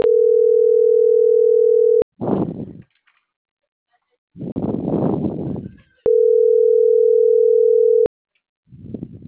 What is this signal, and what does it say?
Ringback tone (North America), 440 + 480 Hz.